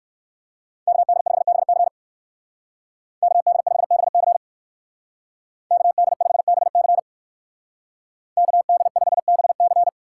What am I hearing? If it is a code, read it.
Morse code, heard as a keyed tone, about 56 wpm, 700 Hz. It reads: KD5BC KD5BC KD5BC KD5BC